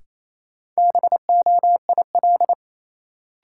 Morse code, heard as a keyed tone, about 28 wpm, 700 Hz. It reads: BOIL
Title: Sets of 5 Words Encore - New Word Drill - 28wpm